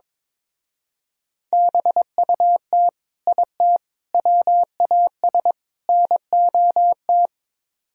Morse code, heard as a keyed tone, about 24 wpm, 700 Hz. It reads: BUT IT WAS NOT
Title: Speed-Racing - Sets of 4 Words 22wpm